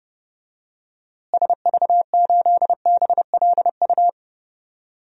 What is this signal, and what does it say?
Morse code, heard as a keyed tone, about 29 wpm, 700 Hz. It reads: SV8BLU
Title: DXCC-100 Call Signs - 6 Characters - Repeated 2x - 30wpm